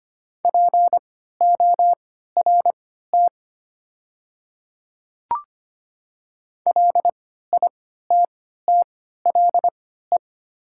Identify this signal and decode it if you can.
Morse code, heard as a keyed tone, about 25 wpm, spaced as for 14 wpm, 700 Hz. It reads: PORT LITTLE
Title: 3x Character Spacing - Top 500 Words - 25wpm